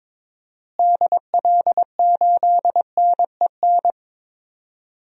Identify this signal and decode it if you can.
Morse code, heard as a keyed tone, about 22 wpm, 700 Hz. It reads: DL8NEN
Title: DXCC-100 Call Signs - 6 Characters - Repeated 3x - 22wpm